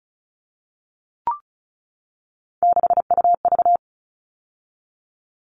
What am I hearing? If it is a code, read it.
Morse code, heard as a keyed tone, about 34 wpm, 700 Hz. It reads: 6UV
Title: All Cut Numbers - 3-Digit 35wpm